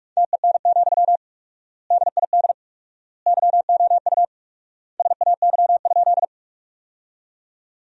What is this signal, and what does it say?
Morse code, heard as a keyed tone, about 43 wpm, 700 Hz. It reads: TEN, DID YOU SAY?
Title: 4x Word-Spacing - Sentences from Top 400 Words - 45wpm